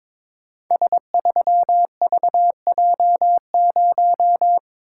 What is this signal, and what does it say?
Morse code, heard as a keyed tone, about 21 wpm, 700 Hz. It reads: S3VJ0
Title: Code Groups - 5 Letter:Numbers - 22wpm